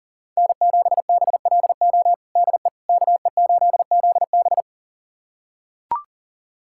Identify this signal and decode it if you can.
Morse code, heard as a keyed tone, about 40 wpm, 700 Hz. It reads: N7BLO DE KE8ZB
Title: USA Callsign DE Callsign - 5 Characters - 40wpm